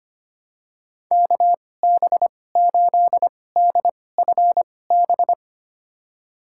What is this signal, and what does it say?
Morse code, heard as a keyed tone, about 25 wpm, spaced as for 18 wpm, 700 Hz. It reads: KB8DFB
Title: USA Call Signs - 6 Characters - 2x Character Spacing - 25wpm